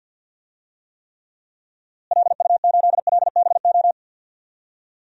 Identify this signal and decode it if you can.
Morse code, heard as a keyed tone, about 47 wpm, 700 Hz. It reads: LU8LBO
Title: DXCC-100 Call Signs - 6 Characters - Repeated 2x - 50wpm